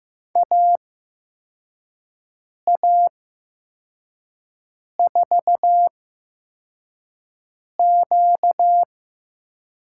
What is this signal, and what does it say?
Morse code, heard as a keyed tone, about 15 wpm, spaced as for 4 wpm, 700 Hz. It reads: AA4Q
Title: USA Call Signs - 5 Characters - 8x Character Spacing - 15wpm